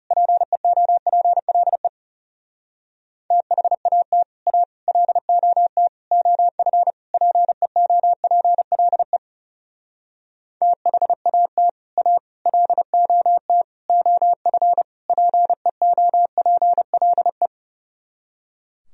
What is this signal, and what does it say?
Morse code, heard as a keyed tone, about 38 wpm, 700 Hz. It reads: PEOPLE THAT A LOT OF PEOPLE THAT A LOT OF PEOPLE